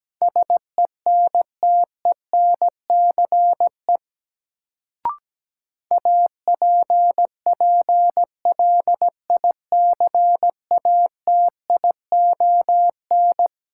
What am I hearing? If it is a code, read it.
Morse code, heard as a keyed tone, about 17 wpm, 700 Hz. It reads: SENTENCE APPLICATION
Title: Sentences from Top 1200 Words - New Word Drill Alt - 17wpm